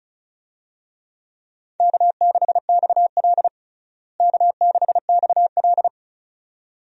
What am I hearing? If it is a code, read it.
Morse code, heard as a keyed tone, about 36 wpm, 700 Hz. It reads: K6XL K6XL
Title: USA Call Signs - 4 Characters - Repeated 2x - 35wpm